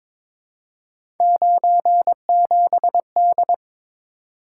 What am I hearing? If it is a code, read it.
Morse code, heard as a keyed tone, about 22 wpm, 700 Hz. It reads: 97D
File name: practice / 22wpm / Three Letter-Number Combinations - 22wpm